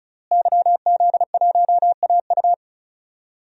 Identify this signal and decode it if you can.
Morse code, heard as a keyed tone, about 36 wpm, 700 Hz. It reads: YZ1AU
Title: DXCC-100 Call Signs - 5 Characters - Repeated 2x - 35wpm